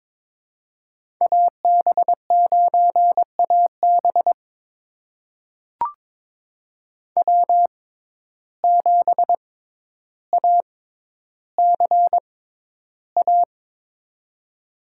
Morse code, heard as a keyed tone, about 21 wpm, spaced as for 11 wpm, 700 Hz. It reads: AB9AB W7ACA
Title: USA Call Signs - 5 Characters - 6x Character Spacing - 22wpm